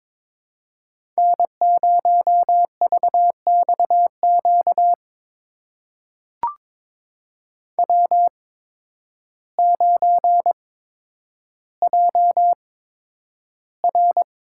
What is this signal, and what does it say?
Morse code, heard as a keyed tone, about 22 wpm, spaced as for 10 wpm, 700 Hz. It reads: N0VXQ W9JR
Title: USA Call Signs - 5 Characters - 8x Character Spacing - 22wpm